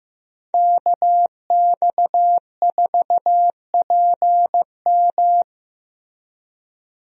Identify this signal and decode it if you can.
Morse code, heard as a keyed tone, about 15 wpm, 700 Hz. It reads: KX4PM
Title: USA Call Signs - 5 Characters - Repeated 3x - 15wpm